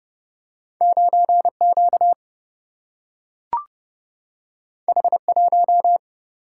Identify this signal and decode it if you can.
Morse code, heard as a keyed tone, about 30 wpm, 700 Hz. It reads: 9 Q H 1
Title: Single-Letter Single-Number 30wpm